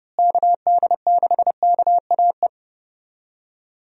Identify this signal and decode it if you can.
Morse code, heard as a keyed tone, about 31 wpm, 700 Hz. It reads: KD6KAE
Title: USA Call Signs - 6 Characters - Repeated 2x - 30wpm